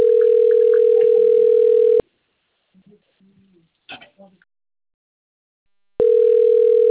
Ringback tone (North America), 440 + 480 Hz.
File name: rg-1600-unknown- - - .49676